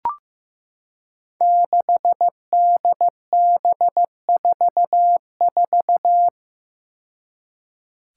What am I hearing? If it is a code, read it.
Morse code, heard as a keyed tone, about 15 wpm, 700 Hz. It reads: 6DB44
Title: Five Challenging Letter-Number Combinations 15wpm